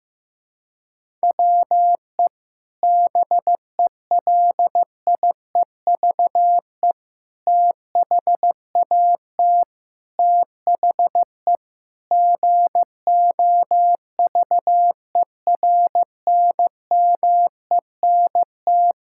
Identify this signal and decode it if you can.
Morse code, heard as a keyed tone, about 16 wpm, 700 Hz. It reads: WE BELIEVE THAT THE GOVERNMENT
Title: Sentences from Top 1000 Words - Review Part 6 - 15wpm